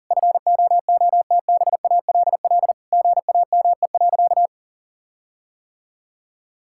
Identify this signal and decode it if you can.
Morse code, heard as a keyed tone, about 40 wpm, 700 Hz. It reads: FOOTBALL GAME.
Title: Sentences from Top 1600 Words 40wpm